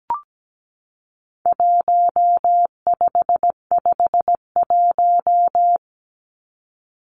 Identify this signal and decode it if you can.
Morse code, heard as a keyed tone, about 17 wpm, 700 Hz. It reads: 1551
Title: Numbers - Contest Style - 4-Digit 17wpm